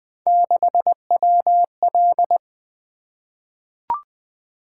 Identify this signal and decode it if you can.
Morse code, heard as a keyed tone, about 20 wpm, 700 Hz. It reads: 6WL
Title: Three Letter-Number Combinations - 20wpm